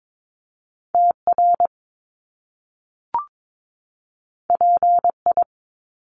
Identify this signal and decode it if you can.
Morse code, heard as a keyed tone, about 22 wpm, 700 Hz. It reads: TR PI
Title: Top Two Letter Combinations in Words 22wpm